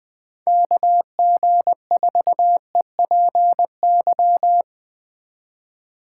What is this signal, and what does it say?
Morse code, heard as a keyed tone, about 20 wpm, 700 Hz. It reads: KG4EPY